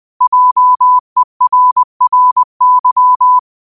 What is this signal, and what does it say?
Morse code, heard as a keyed tone, about 20 wpm, 1000 Hz. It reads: JERRY
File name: qotdfiles / 20wpm